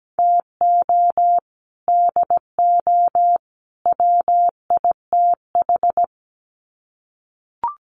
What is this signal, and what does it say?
Morse code, heard as a keyed tone, about 17 wpm, 700 Hz. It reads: TO DO WITH
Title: Sets of 3 Words 17wpm